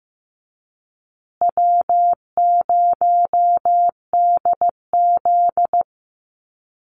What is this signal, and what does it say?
Morse code, heard as a keyed tone, about 15 wpm, 700 Hz. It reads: W0DZ
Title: USA Call Signs Repeated 2x 15wpm